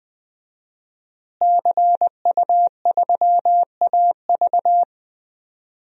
Morse code, heard as a keyed tone, about 20 wpm, 700 Hz. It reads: CU3AV
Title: DXCC-100 Call Signs - 5 Characters - Repeated 2x - 20wpm